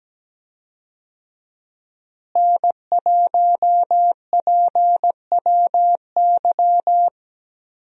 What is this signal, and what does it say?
Morse code, heard as a keyed tone, about 17 wpm, 700 Hz. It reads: N1PWY
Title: USA Call Signs - 5 Characters - Repeated 2x - 17wpm